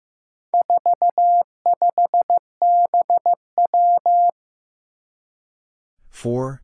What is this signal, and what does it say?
Morse code, heard as a keyed tone, about 15 wpm, 700 Hz. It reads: 45BW
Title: Four Challenging Letter-Number Combinations 15wpm